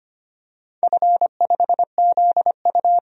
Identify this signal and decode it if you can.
Morse code, heard as a keyed tone, about 25 wpm, 700 Hz. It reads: F5ZU
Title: DXCC-100 Call Signs - 4 Characters - 25wpm